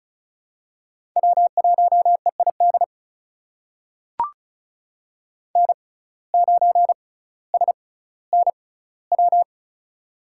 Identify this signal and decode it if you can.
Morse code, heard as a keyed tone, about 34 wpm, spaced as for 18 wpm, 700 Hz. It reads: W1EID N9SNW
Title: USA Call Signs - 5 Characters - 6x Character Spacing - 35wpm